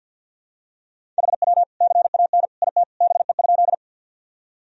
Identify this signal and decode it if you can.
Morse code, heard as a keyed tone, about 46 wpm, 700 Hz. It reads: HW CAN IT BE?
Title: Sentences from Top 100 Words and Abbreviations 50wpm